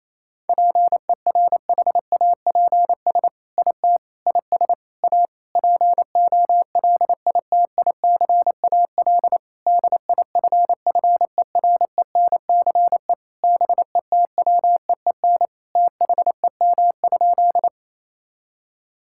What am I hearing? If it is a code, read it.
Morse code, heard as a keyed tone, about 28 wpm, 700 Hz. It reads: PERHAPS IT IS A POLITICAL DIFFERENCE BETWEEN THEM?
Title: Sentences from Top 1000 Words - Review Part 5 - 28wpm